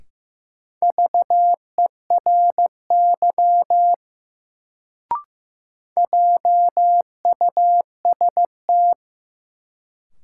Morse code, heard as a keyed tone, about 15 wpm, 700 Hz. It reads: VERY JUST
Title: Sentences from Top 200 Words - New Word Drill - 15wpm